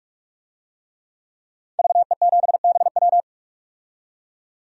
Morse code, heard as a keyed tone, about 46 wpm, 700 Hz. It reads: VE7BW
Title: DXCC-100 Call Signs - 5 Characters - 45wpm